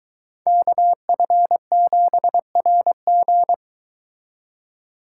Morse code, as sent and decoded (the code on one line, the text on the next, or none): -.- ..-. --... .-. --.
KF7RG